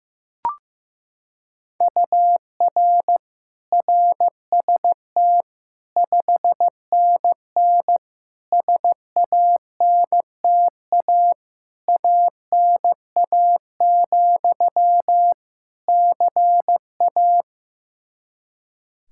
Morse code, heard as a keyed tone, about 15 wpm, 700 Hz. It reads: UR RST 5NN SANTA ANA, CA
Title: Practice QSO - Exchange 1 - Beginning - 15wpm